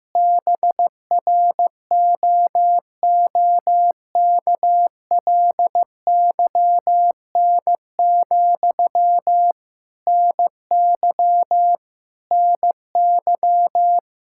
Morse code, heard as a keyed tone, about 16 wpm, 700 Hz. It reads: BROOKLYN, NY NY
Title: Practice QSO - Exchange 1 - Beginning - 15wpm